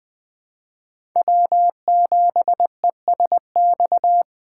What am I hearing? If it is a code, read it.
Morse code, heard as a keyed tone, about 20 wpm, 700 Hz. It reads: W7ESX
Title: USA Call Signs - 5 Characters - 4x Character Spacing - 20wpm